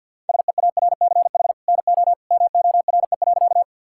Morse code, heard as a keyed tone, about 50 wpm, 700 Hz. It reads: SEARCH NO MORE.
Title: Sentences from Top 1000 Words - Review Part 5 - 50wpm